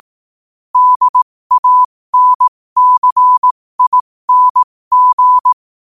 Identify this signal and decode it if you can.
Morse code, heard as a keyed tone, about 18 wpm, spaced as for 15 wpm, 1000 Hz. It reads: DANCING
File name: qotdfiles / 15wpm